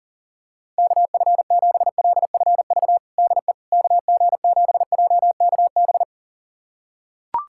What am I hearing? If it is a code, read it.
Morse code, heard as a keyed tone, about 37 wpm, 700 Hz. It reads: KF7LFV DE KG7JKB
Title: USA Callsign DE Callsign 40wpm